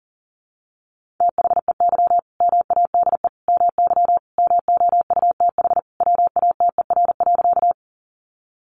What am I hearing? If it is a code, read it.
Morse code, heard as a keyed tone, about 40 wpm, 700 Hz. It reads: THEY MADE MY MOUTH WATER.
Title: Sentences from Top 1500 Words 40wpm